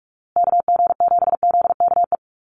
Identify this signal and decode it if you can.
Morse code, heard as a keyed tone, about 44 wpm, 700 Hz. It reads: KG7ZKE
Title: USA Call Signs Repeated 3x 45wpm